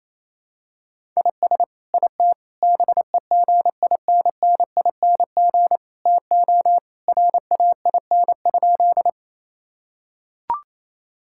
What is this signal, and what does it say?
Morse code, heard as a keyed tone, about 27 wpm, 700 Hz. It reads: IS IT BEGINNING TO RAIN?
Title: Sentences from Top 1000 Words - Review Part 5 - 28wpm